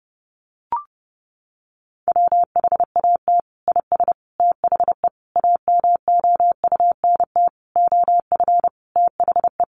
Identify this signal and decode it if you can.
Morse code, heard as a keyed tone, about 30 wpm, 700 Hz. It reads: WHAT IS THE AMOUNT OF THE
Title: Sentences from Top 1500 Words 30wpm